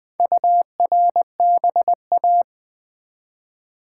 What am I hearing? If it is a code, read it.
Morse code, heard as a keyed tone, about 20 wpm, 700 Hz. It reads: URBA
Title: Build A Word - 1 Letter At A Time - 5-Letter Words - 20wpm